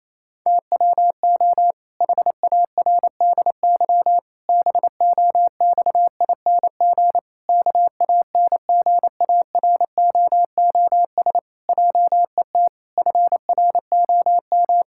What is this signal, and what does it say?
Morse code, heard as a keyed tone, about 28 wpm, 700 Hz. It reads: TWO HARDY BOXING KANGAROOS JET FROM
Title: Pangram Challenge v2 - 28wpm